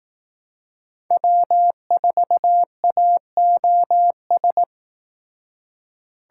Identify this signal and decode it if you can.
Morse code, heard as a keyed tone, about 18 wpm, 700 Hz. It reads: W4AOS
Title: Speed Racing - USA 1x3 Call Signs 15wpm